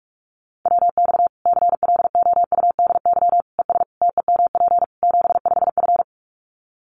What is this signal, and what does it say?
Morse code, heard as a keyed tone, about 44 wpm, 700 Hz. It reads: WX CLOUDY ES TEMP 75F
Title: Common QSO Phrases 45wpm